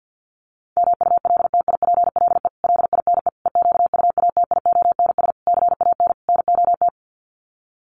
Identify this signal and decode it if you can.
Morse code, heard as a keyed tone, about 50 wpm, 700 Hz. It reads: MULTIPLE LINE EQUATIONS CAN NOT
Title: Sets of 5 Words 50wpm